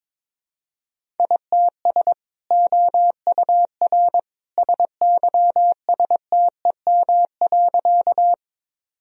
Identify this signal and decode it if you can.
Morse code, heard as a keyed tone, about 22 wpm, 700 Hz. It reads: ITS OUR SYSTEM.
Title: Sentences from Top 1000 Words - Review Part 3 - 22wpm